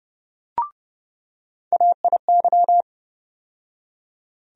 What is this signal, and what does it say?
Morse code, heard as a keyed tone, about 31 wpm, 700 Hz. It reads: AIY
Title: USA Call Sign Postfixes - 3 Letters 30wpm